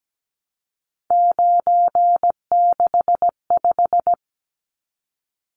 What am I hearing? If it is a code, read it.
Morse code, heard as a keyed tone, about 17 wpm, 700 Hz. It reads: 965
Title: Numbers - 3-Digit 17wpm